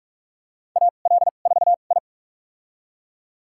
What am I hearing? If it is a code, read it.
Morse code, heard as a keyed tone, about 41 wpm, spaced as for 31 wpm, 700 Hz. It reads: AL4I
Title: USA Call Signs - 4 Characters - 2x Character Spacing - 45wpm